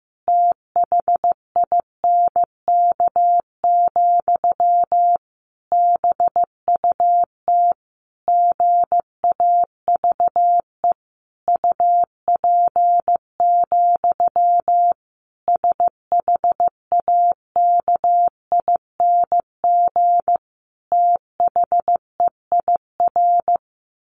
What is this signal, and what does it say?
Morse code, heard as a keyed tone, about 15 wpm, 700 Hz. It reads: THINK, BUT GAVE UP, SHAKING THEIR
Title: A Sound Of Thunder 15wpm